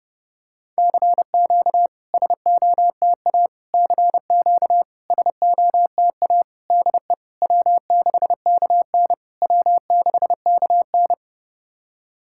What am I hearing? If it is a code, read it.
Morse code, heard as a keyed tone, about 32 wpm, 700 Hz. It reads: CQ SOTA CQ SOTA DE W6KN W6KN
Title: SOTA QSO Practice 30wpm